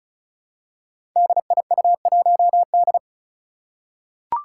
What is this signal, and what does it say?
Morse code, heard as a keyed tone, about 36 wpm, 700 Hz. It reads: DIU1D
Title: Five Challenging Letter-Number Combinations 35wpm